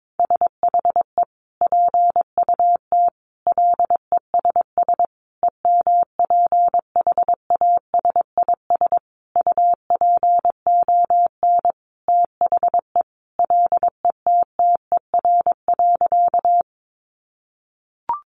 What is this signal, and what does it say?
Morse code, heard as a keyed tone, about 22 wpm, 700 Hz. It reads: SHE PUT LESS EMPHASIS UPON THE LETTER.
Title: Sentences from Top 1400 Words 22wpm